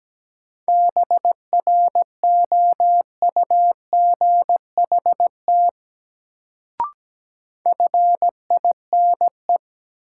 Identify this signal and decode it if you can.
Morse code, heard as a keyed tone, about 17 wpm, 700 Hz. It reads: BROUGHT FINE
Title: Sentences from Top 500 Words - New Word Drill - 17wpm